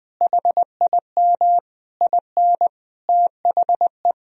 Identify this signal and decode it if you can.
Morse code, heard as a keyed tone, about 20 wpm, 700 Hz. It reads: HIM IN THE
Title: Sentences from Top 1000 Words - Review Part 3 - 20wpm